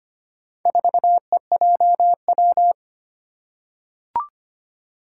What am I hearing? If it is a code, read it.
Morse code, heard as a keyed tone, about 25 wpm, 700 Hz. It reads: 4EJW
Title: Four Challenging Letter-Number Combinations 25wpm